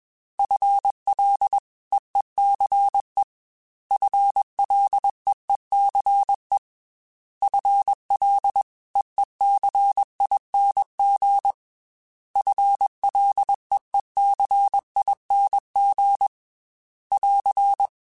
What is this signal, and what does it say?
Morse code, heard as a keyed tone, about 21 wpm, 800 Hz. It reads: FLEECE FLEECE FLEECING FLEECING <AR>
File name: FL / FL mp3s / FL Prefixes 3x2 sp @20